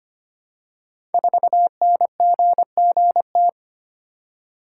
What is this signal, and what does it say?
Morse code, heard as a keyed tone, about 25 wpm, 700 Hz. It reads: 4NGGT
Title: Code Groups - 5 Letter:Numbers - 25wpm